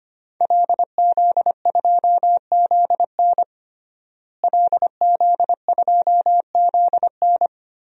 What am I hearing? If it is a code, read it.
Morse code, heard as a keyed tone, about 25 wpm, 700 Hz. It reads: LZ2ZN LZ2ZN
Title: DXCC-100 Call Signs - 5 Characters - Repeated 3x - 25wpm